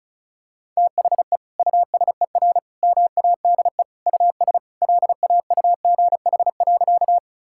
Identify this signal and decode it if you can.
Morse code, heard as a keyed tone, about 36 wpm, 700 Hz. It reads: THE USER MADE US LAUGH.
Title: Sentences from Top 1000 Words - Review Part 5 - 35wpm